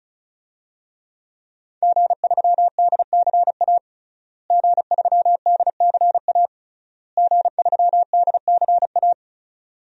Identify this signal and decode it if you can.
Morse code, heard as a keyed tone, about 36 wpm, 700 Hz. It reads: G3DCA G3DCA G3DCA
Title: DXCC-100 Call Signs - 5 Characters - Repeated 3x - 35wpm